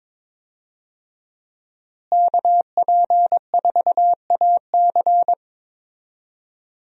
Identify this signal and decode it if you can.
Morse code, heard as a keyed tone, about 21 wpm, 700 Hz. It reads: KP4AC
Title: USA Call Signs - 5 Characters - Repeated 3x - 22wpm